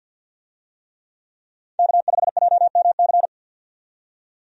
Morse code, heard as a keyed tone, about 46 wpm, 700 Hz. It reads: K5JMC
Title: USA Call Signs - 5 Characters - 50wpm